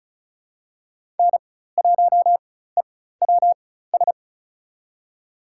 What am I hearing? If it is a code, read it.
Morse code, heard as a keyed tone, about 36 wpm, spaced as for 16 wpm, 700 Hz. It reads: N1EWS
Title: USA Call Signs - 5 Characters - 4x Character Spacing - 35wpm